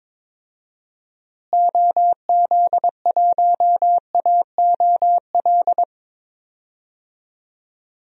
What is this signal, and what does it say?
Morse code, heard as a keyed tone, about 22 wpm, 700 Hz. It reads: OZ1AOL
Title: DXCC-100 Call Signs - 6 Characters - Repeated 2x - 22wpm